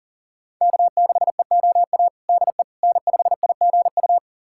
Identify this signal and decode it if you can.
Morse code, heard as a keyed tone, about 37 wpm, 700 Hz. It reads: K6EOA DE N5IGU
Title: USA Callsign DE Callsign - 5 Characters - 40wpm